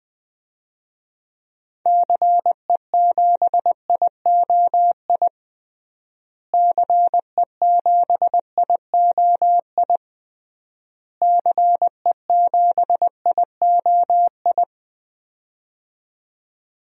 Morse code, heard as a keyed tone, about 20 wpm, 700 Hz. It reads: CE7IOI CE7IOI CE7IOI
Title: DXCC-100 Call Signs - 6 Characters - Repeated 3x - 20wpm